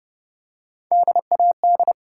Morse code, heard as a keyed tone, about 30 wpm, 700 Hz. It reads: DAD
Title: Three Letter Words - 30wpm